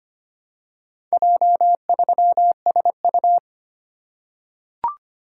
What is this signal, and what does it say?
Morse code, heard as a keyed tone, about 25 wpm, 700 Hz. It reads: J3SU
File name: practice / 25wpm / Four Letter-Number Combinations - 25wpm